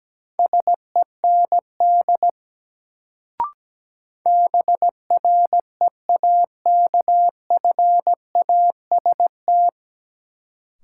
Morse code, heard as a keyed tone, about 17 wpm, 700 Hz. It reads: SEND BREAKFAST
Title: Sentences from Top 1500 Words - New Word Drill Alt - 17wpm